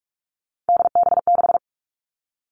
Morse code, heard as a keyed tone, about 46 wpm, 700 Hz. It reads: DB6
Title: Three Challenging Letter-Number Combinations 45wpm